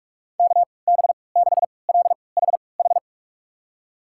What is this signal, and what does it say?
Morse code, heard as a keyed tone, about 42 wpm, spaced as for 25 wpm, 700 Hz. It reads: KB6LHH
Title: USA Call Signs - 3x Character Spacing - 45wpm